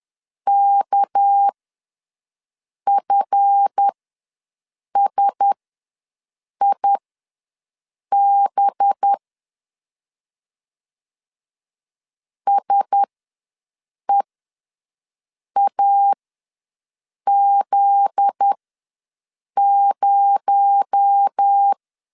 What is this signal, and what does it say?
Morse code, heard as a keyed tone, about 10 wpm, spaced as for 6 wpm, 800 Hz. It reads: KFSIB SEAZ0
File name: LEZIONI / 7 GF9Z / 7 GF9Z ws 10wpm 3919